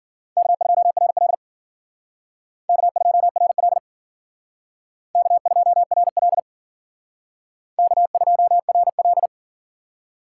Morse code, heard as a keyed tone, about 51 wpm, 700 Hz. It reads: K2RL K2RL K2RL K2RL